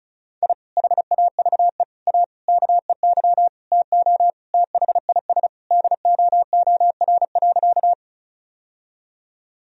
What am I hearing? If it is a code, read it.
Morse code, heard as a keyed tone, about 36 wpm, 700 Hz. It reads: I HAVE A KEY TO THIS DOOR.
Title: Sentences from Top 1000 Words - Review Part 4 - 35wpm